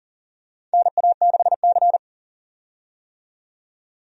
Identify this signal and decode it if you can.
Morse code, heard as a keyed tone, about 38 wpm, 700 Hz. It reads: NA6C
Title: USA Call Signs - 4 Characters - 40wpm